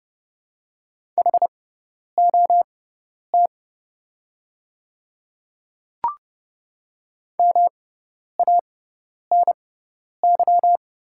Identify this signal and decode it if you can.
Morse code, heard as a keyed tone, about 29 wpm, spaced as for 10 wpm, 700 Hz. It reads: HOT MANY
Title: 6x Character Spacing - Top 300 Words - 30wpm